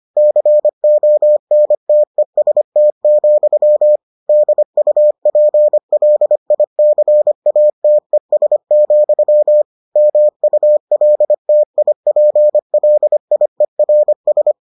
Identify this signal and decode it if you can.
Morse code, heard as a keyed tone, about 25 wpm, 600 Hz. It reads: CONTEST, DUPLICATES, MULTIPLIERS